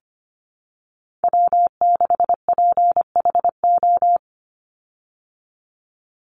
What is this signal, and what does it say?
Morse code, heard as a keyed tone, about 25 wpm, 700 Hz. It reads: W6PHO
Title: Call Signs Repeated 3x - 25wpm